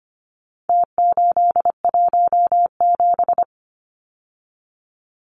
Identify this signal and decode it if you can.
Morse code, heard as a keyed tone, about 25 wpm, 700 Hz. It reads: T817
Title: Numbers - Contest Style - 4-Digit 25wpm